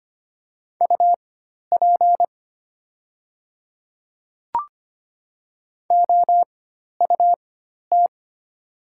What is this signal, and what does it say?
Morse code, heard as a keyed tone, about 25 wpm, spaced as for 12 wpm, 700 Hz. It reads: UP OUT
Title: 4x Character Spacing - Top 200 Words - 25wpm